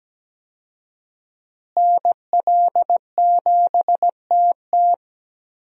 Morse code, heard as a keyed tone, about 17 wpm, 700 Hz. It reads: NL7TT
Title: USA Call Signs - 5 Characters - Repeated 2x - 17wpm